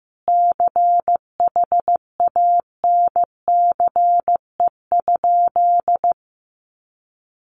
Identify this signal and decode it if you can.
Morse code, heard as a keyed tone, about 15 wpm, 700 Hz. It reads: CHANCE?
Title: Sentences from Top 800 Words 15wpm